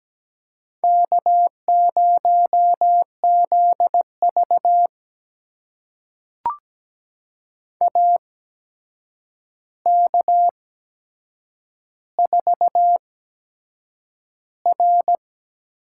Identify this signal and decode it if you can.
Morse code, heard as a keyed tone, about 17 wpm, spaced as for 7 wpm, 700 Hz. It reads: K0ZV AK4R
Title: USA Call Signs - 4 Characters - 8x Character Spacing - 17wpm